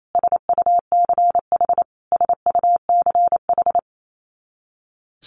Morse code, heard as a keyed tone, about 28 wpm, 700 Hz. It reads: SUCH SUCH
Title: Top 200 Words Repeated 28wpm